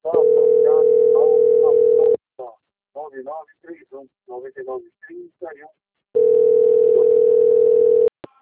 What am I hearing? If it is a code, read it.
Ringback tone (North America), 440 + 480 Hz, with speech.
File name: out- - - - - .1223